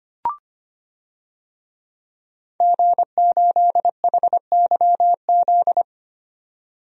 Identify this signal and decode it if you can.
Morse code, heard as a keyed tone, about 25 wpm, 700 Hz. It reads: G8HYZ
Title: DXCC-100 Call Signs - 5 Characters - 25wpm